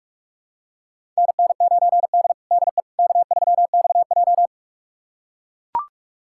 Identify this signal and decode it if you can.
Morse code, heard as a keyed tone, about 44 wpm, 690 Hz. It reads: NN9D DE K3XJ
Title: USA Callsign DE Callsign - 4 Characters - 45wpm